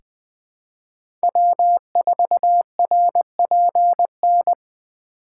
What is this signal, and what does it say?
Morse code, heard as a keyed tone, about 20 wpm, 700 Hz. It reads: W4RPN
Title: USA Call Signs - 6x Character Spacing - 20wpm